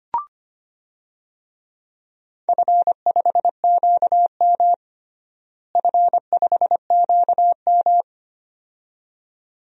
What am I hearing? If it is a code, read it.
Morse code, heard as a keyed tone, about 25 wpm, 700 Hz. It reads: F5QM F5QM
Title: DXCC-100 Call Signs - 4 Characters - Repeated 2x - 25wpm